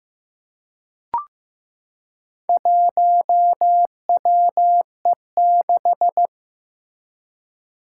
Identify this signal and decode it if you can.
Morse code, heard as a keyed tone, about 15 wpm, 700 Hz. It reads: 1WE6
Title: Four Challenging Letter-Number Combinations 15wpm